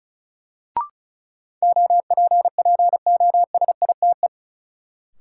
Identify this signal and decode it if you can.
Morse code, heard as a keyed tone, about 34 wpm, 700 Hz. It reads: OPPOSITE
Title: Sets of 5 Words Encore - New Word Drill - 35wpm